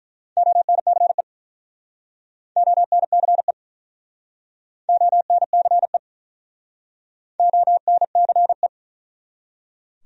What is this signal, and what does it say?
Morse code, heard as a keyed tone, about 48 wpm, 700 Hz. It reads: ONCE ONCE ONCE ONCE